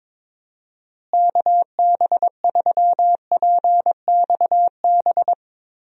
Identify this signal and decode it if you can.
Morse code, heard as a keyed tone, about 22 wpm, 700 Hz. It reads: KB3PXB
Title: USA Call Signs - 6 Characters - Repeated 2x - 22wpm